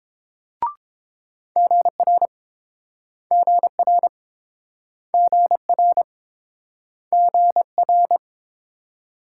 Morse code, heard as a keyed tone, about 28 wpm, 700 Hz. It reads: GR GR GR GR